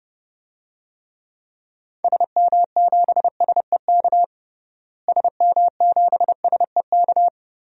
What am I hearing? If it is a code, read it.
Morse code, heard as a keyed tone, about 30 wpm, 700 Hz. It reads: SM7SEK SM7SEK
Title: DXCC-100 Call Signs - 6 Characters - Repeated 3x - 30wpm